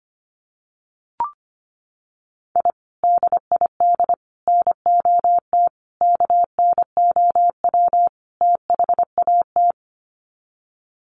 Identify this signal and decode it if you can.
Morse code, heard as a keyed tone, about 25 wpm, 700 Hz. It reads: I DID NOT KNOW THAT
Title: Sets of 5 Words 25wpm